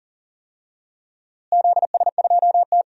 Morse code, heard as a keyed tone, about 40 wpm, 700 Hz. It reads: ZS2T
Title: DXCC-100 Call Signs - 4 Characters - Repeated 3x - 40wpm